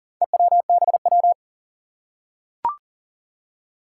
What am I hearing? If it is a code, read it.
Morse code, heard as a keyed tone, about 40 wpm, 700 Hz. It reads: EWBW
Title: Four Challenging Letter-Number Combinations 40wpm